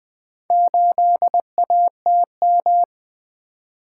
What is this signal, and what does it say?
Morse code, heard as a keyed tone, about 20 wpm, 700 Hz. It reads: 8ATM
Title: Four Letter-Number Combinations - 20wpm